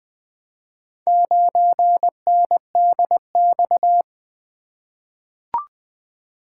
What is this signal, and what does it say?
Morse code, heard as a keyed tone, about 20 wpm, 700 Hz. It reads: 9NDX
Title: Four Letter-Number Combinations - 20wpm